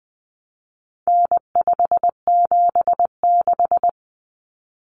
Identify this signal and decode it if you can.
Morse code, heard as a keyed tone, about 20 wpm, 700 Hz. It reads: N576
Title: Common Cut Numbers - 4-Digit 20wpm